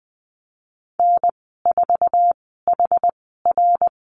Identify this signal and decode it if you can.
Morse code, heard as a keyed tone, about 20 wpm, spaced as for 14 wpm, 700 Hz. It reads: N4HR
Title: 062 Lesson 11g - Letter h - Call Signs 20wpm